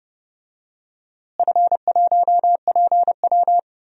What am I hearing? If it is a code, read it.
Morse code, heard as a keyed tone, about 30 wpm, 700 Hz. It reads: F1PW
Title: DXCC-100 Call Signs - 4 Characters - Repeated 2x - 30wpm